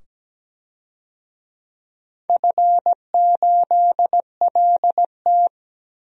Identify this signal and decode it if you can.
Morse code, heard as a keyed tone, about 17 wpm, 700 Hz. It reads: F8LT
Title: DXCC-100 Call Signs - 4 Characters - 17wpm